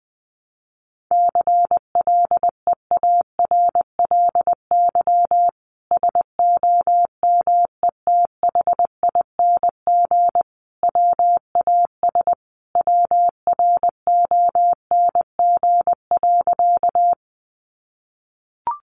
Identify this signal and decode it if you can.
Morse code, heard as a keyed tone, about 20 wpm, 700 Hz. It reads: CLEARLY SOMETHING WAS WRONG.
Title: Sentences from Top 1300 Words 20wpm